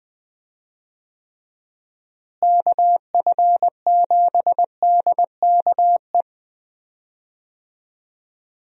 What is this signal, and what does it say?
Morse code, heard as a keyed tone, about 21 wpm, 700 Hz. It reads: KF7DKE
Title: USA Call Signs - 6 Characters - Repeated 3x - 20wpm